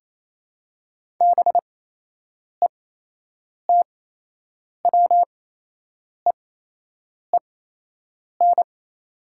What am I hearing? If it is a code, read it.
Morse code, heard as a keyed tone, about 28 wpm, spaced as for 8 wpm, 690 Hz. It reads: BETWEEN
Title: 8x Character Spacing - Top 500 Words - 28wpm